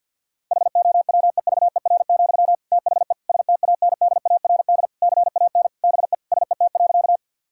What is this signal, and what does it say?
Morse code, heard as a keyed tone, about 50 wpm, 700 Hz. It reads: HOWEVER, THE STANDARD CAN BE SET.
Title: Sentences from Top 1000 Words - Review Part 6 - 50wpm